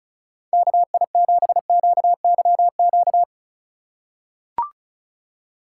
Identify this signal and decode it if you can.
Morse code, heard as a keyed tone, about 36 wpm, 700 Hz. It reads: KI7QYQ
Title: USA Call Signs - 6 Characters - 6x Character Spacing - 35wpm